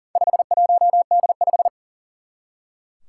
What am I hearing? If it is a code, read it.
Morse code, heard as a keyed tone, about 40 wpm, 700 Hz. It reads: 51D5
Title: Four Challenging Letter-Number Combinations 40wpm